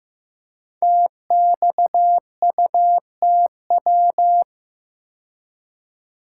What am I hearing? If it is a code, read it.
Morse code, heard as a keyed tone, about 15 wpm, 700 Hz. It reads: TXUTW
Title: Code Groups - 5 Letters - 15wpm